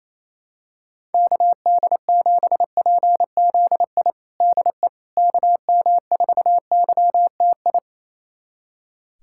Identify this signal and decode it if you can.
Morse code, heard as a keyed tone, about 28 wpm, 700 Hz. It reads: KD7PZI DE KM4YTI
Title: USA Callsign DE Callsign - 6 Characters - 28wpm